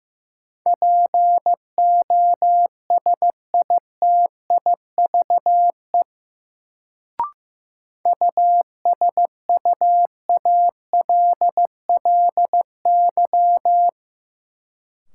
Morse code, heard as a keyed tone, about 15 wpm, 700 Hz. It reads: POSITIVE USUALLY
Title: Sentences from Top 1400 Words - New Word Drill Alt - 15wpm